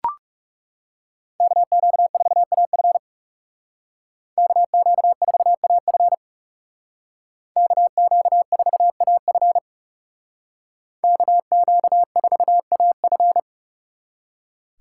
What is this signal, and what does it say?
Morse code, heard as a keyed tone, about 38 wpm, 700 Hz. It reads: KQ4AF KQ4AF KQ4AF KQ4AF